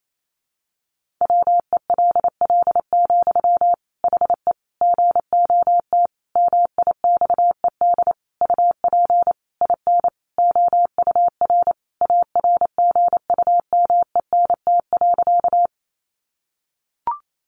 Morse code, heard as a keyed tone, about 28 wpm, 700 Hz. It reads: WELL, HE GOT MIXED UP IN OUR ARGUMENT.
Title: Sentences from Top 1500 Words 28wpm